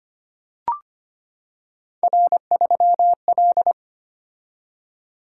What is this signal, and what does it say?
Morse code, heard as a keyed tone, about 25 wpm, 700 Hz. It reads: R3L
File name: Three Letter-Number Combinations - 25wpm